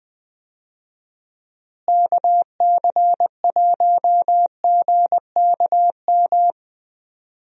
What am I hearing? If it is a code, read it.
Morse code, heard as a keyed tone, about 20 wpm, 700 Hz. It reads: KC1GKM
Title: USA Call Signs - 6 Characters - Repeated 2x - 20wpm